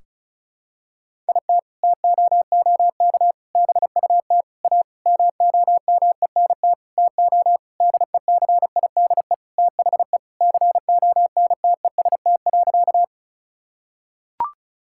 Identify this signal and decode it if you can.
Morse code, heard as a keyed tone, about 36 wpm, 700 Hz. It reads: IT TOOK BUT A MOMENT TO DECIDE THE CONTEST.
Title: Sentences from Top 1600 Words 35wpm